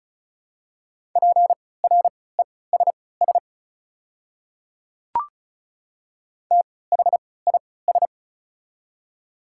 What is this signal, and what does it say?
Morse code, heard as a keyed tone, about 34 wpm, spaced as for 20 wpm, 700 Hz. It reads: PRESS THIS
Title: 3x Character Spacing - Top 300 Words - 35wpm